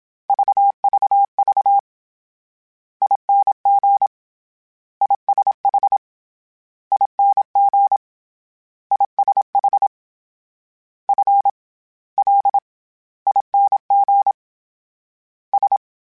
Morse code, heard as a keyed tone, about 27 wpm, 800 Hz. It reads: VVV ING ISH ING ISH F L ING S